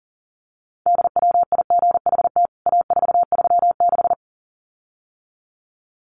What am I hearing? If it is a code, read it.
Morse code, heard as a keyed tone, about 40 wpm, 700 Hz. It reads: DWIGHT A436
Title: CWT Exchange 40wpm